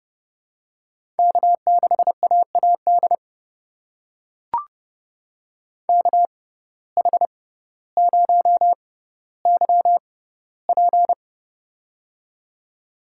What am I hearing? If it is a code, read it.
Morse code, heard as a keyed tone, about 30 wpm, spaced as for 15 wpm, 700 Hz. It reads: K6AAD KH0YP
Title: USA Call Signs - 5 Characters - 6x Character Spacing - 30wpm